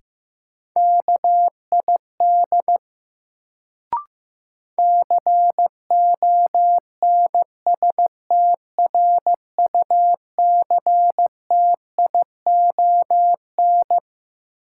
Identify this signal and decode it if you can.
Morse code, heard as a keyed tone, about 15 wpm, 700 Hz. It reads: KID CONSTRUCTION
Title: Sentences from Top 1600 Words - New Word Drill Alt - 15wpm